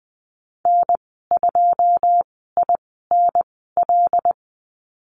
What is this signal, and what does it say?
Morse code, heard as a keyed tone, about 20 wpm, spaced as for 15 wpm, 700 Hz. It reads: N2INL
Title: 082 Lesson 14g - Number 2 - Call Signs 20wpm